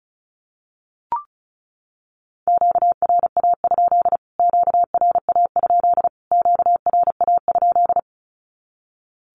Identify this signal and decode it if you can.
Morse code, heard as a keyed tone, about 34 wpm, 700 Hz. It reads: QRA? QRA? QRA?
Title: Q-Codes Repeated 3x 35wpm